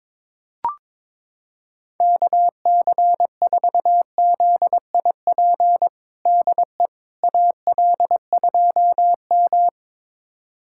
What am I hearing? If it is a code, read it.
Morse code, heard as a keyed tone, about 22 wpm, 700 Hz. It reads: KC4ZIP DE AL2M
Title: USA Callsign DE Callsign 22wpm